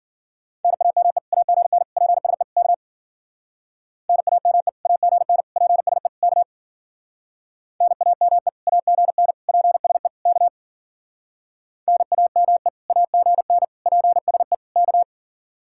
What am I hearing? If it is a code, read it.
Morse code, heard as a keyed tone, about 52 wpm, 700 Hz. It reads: NAME AGN PSE K NAME AGN PSE K NAME AGN PSE K NAME AGN PSE K